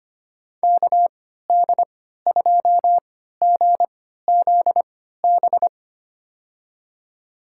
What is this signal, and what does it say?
Morse code, heard as a keyed tone, about 25 wpm, spaced as for 14 wpm, 700 Hz. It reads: KD2GZB
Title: USA Call Signs - 6 Characters - 3x Character Spacing - 25wpm